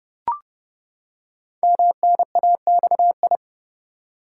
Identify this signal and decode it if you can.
Morse code, heard as a keyed tone, about 30 wpm, 700 Hz. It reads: MNAXI
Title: Code Groups - 5 Letters - 30wpm